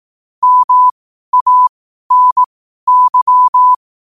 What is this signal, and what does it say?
Morse code, heard as a keyed tone, about 18 wpm, spaced as for 12 wpm, 1000 Hz. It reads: MANY
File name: qotdfiles / 12wpm